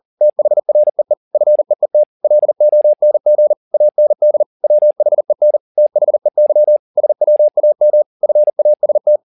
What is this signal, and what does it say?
Morse code, heard as a keyed tone, about 40 wpm, 600 Hz. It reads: THREE FEET LONG AND WHEN THEY SWAM FAST
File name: the old man and the sea 2 40wpm